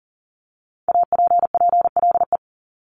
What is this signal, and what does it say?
Morse code, heard as a keyed tone, about 42 wpm, 700 Hz. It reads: APPLE
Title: Multi-letter Phonemes 40wpm